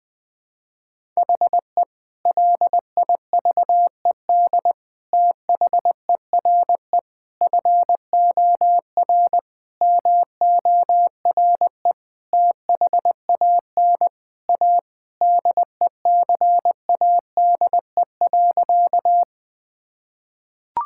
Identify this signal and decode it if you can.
Morse code, heard as a keyed tone, about 20 wpm, 700 Hz. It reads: HE LIVED THERE FOR MORE THAN A DECADE.
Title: Sentences with 5-Word Phrases - 20wpm